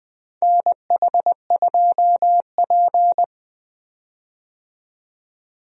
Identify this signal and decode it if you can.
Morse code, heard as a keyed tone, about 20 wpm, 700 Hz. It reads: NH2P
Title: USA Call Signs - 4 Characters - 20wpm